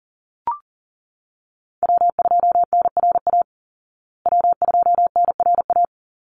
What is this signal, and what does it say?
Morse code, heard as a keyed tone, about 37 wpm, 700 Hz. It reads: W2NRA W2NRA
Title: Call Signs Repeated 3x - 40wpm